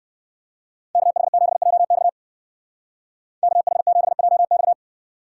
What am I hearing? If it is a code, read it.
Morse code, heard as a keyed tone, about 59 wpm, 700 Hz. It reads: KH7JX KH7JX